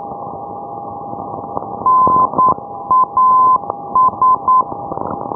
Morse code, heard as a keyed tone, about 9 wpm, 1020 Hz, with noise. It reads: NAS
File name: ndb punta arenas